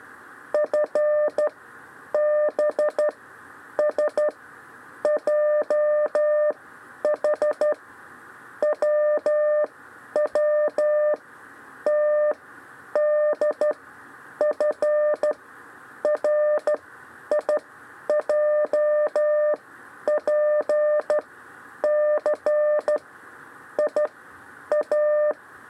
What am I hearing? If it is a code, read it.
Morse code, heard as a keyed tone, about 13 wpm, spaced as for 9 wpm, 600 Hz. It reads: FBSJH WWTDFRIJPC IA
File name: 6 lezione cw 2